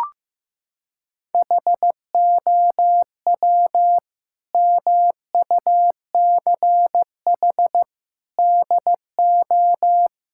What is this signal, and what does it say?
Morse code, heard as a keyed tone, about 15 wpm, 700 Hz. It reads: HOW MUCH DO
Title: Sentences from Top 1000 Words - Review Part 6 - 15wpm